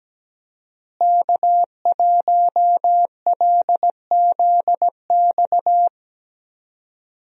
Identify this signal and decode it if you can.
Morse code, heard as a keyed tone, about 17 wpm, 700 Hz. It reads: K1LZX
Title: USA Call Signs - 5 Characters - Repeated 2x - 17wpm